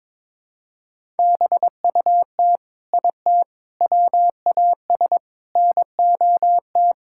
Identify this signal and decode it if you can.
Morse code, heard as a keyed tone, about 22 wpm, 700 Hz. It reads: BUT IT WAS NOT
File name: practice / 22wpm / Sentences with 4-Word Phrases - 22wpm